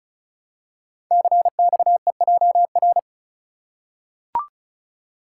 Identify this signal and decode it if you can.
Morse code, heard as a keyed tone, about 36 wpm, 700 Hz. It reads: CXEJR
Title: Code Groups - 5 Letter:Numbers - 35wpm